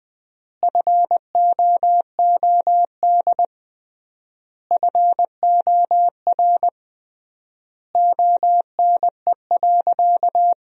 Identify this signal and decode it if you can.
Morse code, heard as a keyed tone, about 20 wpm, 700 Hz. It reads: FOOD FOR ONE.
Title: 3x Word-Spacing - Sentences from Top 300 Words - 20wpm